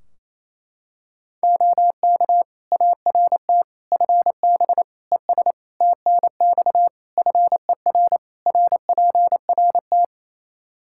Morse code, heard as a keyed tone, about 28 wpm, 700 Hz. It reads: OK ART FB ES TNX FER RPRT
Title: Practice QSO - Exchange 2 - Info - 28wpm